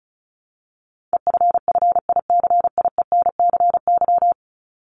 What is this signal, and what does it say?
Morse code, heard as a keyed tone, about 34 wpm, 700 Hz. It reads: EFFICIENCY
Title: Top 1501-1750 Words 35wpm